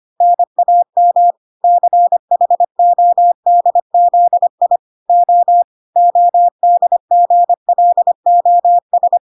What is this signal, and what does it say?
Morse code, heard as a keyed tone, about 25 wpm, 700 Hz. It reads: NAM CHODZI O ODGLOS
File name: 25wpm